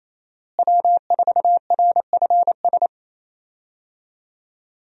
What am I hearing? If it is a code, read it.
Morse code, heard as a keyed tone, about 28 wpm, 700 Hz. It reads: W4RFS
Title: USA Call Signs - 5 Characters - Repeated 2x - 28wpm